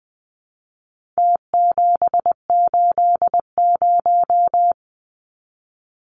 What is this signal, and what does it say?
Morse code, heard as a keyed tone, about 20 wpm, 700 Hz. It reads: T780
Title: Numbers - Contest Style - 4-Digit 20wpm